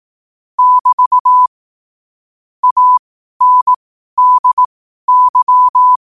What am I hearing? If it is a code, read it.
Morse code, heard as a keyed tone, about 18 wpm, spaced as for 12 wpm, 1000 Hz. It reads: <BT> ANDY
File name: 12wpm